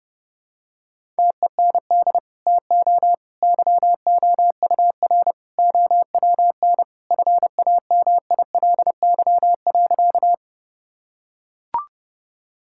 Morse code, heard as a keyed tone, about 30 wpm, 700 Hz. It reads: TEND TO YOUR OWN FAMILY.
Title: Sentences from Top 1000 Words - Review Part 5 - 30wpm